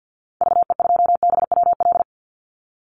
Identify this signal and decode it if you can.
Morse code, heard as a keyed tone, about 50 wpm, 700 Hz. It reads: VE2BWL
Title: Call Signs - 50wpm